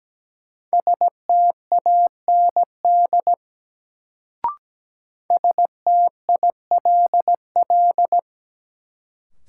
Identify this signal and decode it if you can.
Morse code, heard as a keyed tone, about 17 wpm, 700 Hz. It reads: STAND STILL
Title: Sentences from Top 300 Words - New Word Drill - 17wpm